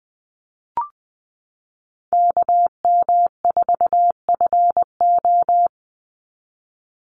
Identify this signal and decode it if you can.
Morse code, heard as a keyed tone, about 20 wpm, 700 Hz. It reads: KM4FO
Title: Call Signs - 20wpm